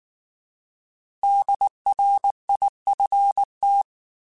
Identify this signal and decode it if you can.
Morse code, heard as a keyed tone, about 19 wpm, 800 Hz. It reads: DRIFT